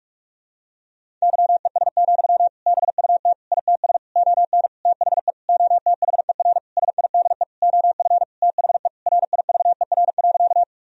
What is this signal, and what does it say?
Morse code, heard as a keyed tone, about 41 wpm, 700 Hz. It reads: YES, BUT ITS ON THE OTHER SIDE OF THE RIVER.
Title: Sentences from Top 1000 Words - Review Part 1 - 45wpm